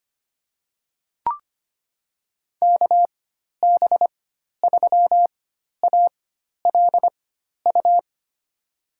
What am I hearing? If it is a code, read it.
Morse code, heard as a keyed tone, about 25 wpm, spaced as for 12 wpm, 700 Hz. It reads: KB3ALU
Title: USA Call Signs - 6 Characters - 4x Character Spacing - 25wpm